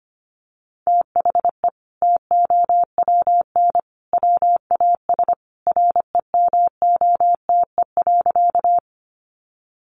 Morse code, heard as a keyed tone, about 25 wpm, 700 Hz. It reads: THE TOWN WAS REMOTE.
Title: Sentences from Top 1300 Words 25wpm